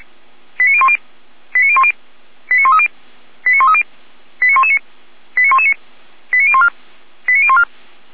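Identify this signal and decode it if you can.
Five-tone selective call (ZVEI), heard as quick runs of five tones, tones of about 70 ms. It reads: 89010 89010 89130 89130 89119 89119 89014 89014
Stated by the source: ZVEI five-tone selective call received on an SDR, recorded by the feuerwehr-orlamuende.org fire brigade server.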